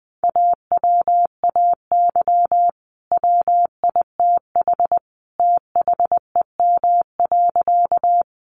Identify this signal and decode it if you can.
Morse code, heard as a keyed tone, about 20 wpm, 700 Hz. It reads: AWAY WITH THEM.
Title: Sentences from Top 600 Words 20wpm